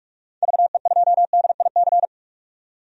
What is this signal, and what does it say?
Morse code, heard as a keyed tone, about 47 wpm, 700 Hz. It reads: VE2DIC
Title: DXCC-100 Call Signs - 6 Characters - 45wpm